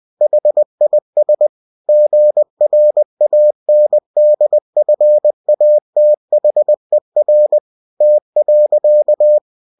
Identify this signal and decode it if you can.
Morse code, heard as a keyed tone, about 20 wpm, 600 Hz. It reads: HIS GRANDFATHER T.
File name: cw-books / winnie-the-pooh / winnie-20wpm-page-0023